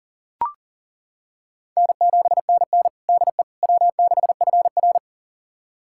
Morse code, heard as a keyed tone, about 40 wpm, 700 Hz. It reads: N7NN DE W6FR
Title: USA Callsign DE Callsign - 4 Characters - 40wpm